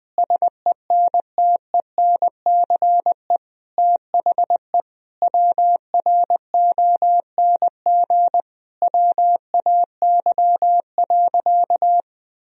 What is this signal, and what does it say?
Morse code, heard as a keyed tone, about 20 wpm, 700 Hz. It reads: SENTENCE THE WRONG WAY.
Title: Sentences from Top 1000 Words - Review Part 1 - 20wpm